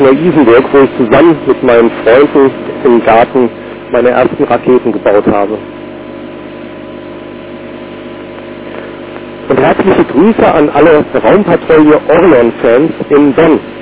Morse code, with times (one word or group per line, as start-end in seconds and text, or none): none